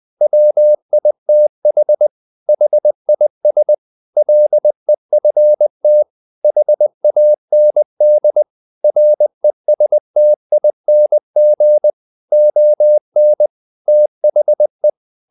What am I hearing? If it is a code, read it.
Morse code, heard as a keyed tone, about 20 wpm, 600 Hz. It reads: WITH HIS LEFT HAND RESTING ON THE
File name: the old man and the sea part 2 20wpm